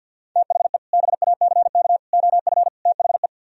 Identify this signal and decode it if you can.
Morse code, heard as a keyed tone, about 64 wpm, 700 Hz. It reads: THE BACK OF THE